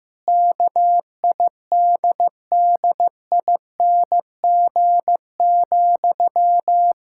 Morse code, heard as a keyed tone, about 15 wpm, 700 Hz. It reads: KIDDING,
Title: Pangram Challenge v2 - 15wpm